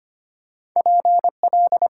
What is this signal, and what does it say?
Morse code, heard as a keyed tone, about 25 wpm, 690 Hz. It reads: PL
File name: Top 90 QSO Elements - 8x Character Spacing - 25wpm